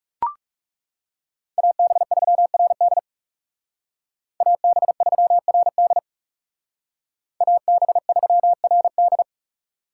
Morse code, heard as a keyed tone, about 41 wpm, 700 Hz. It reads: AB3RD AB3RD AB3RD